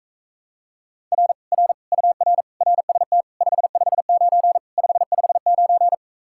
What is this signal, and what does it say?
Morse code, heard as a keyed tone, about 45 wpm, 700 Hz. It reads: R R UR RST 559 559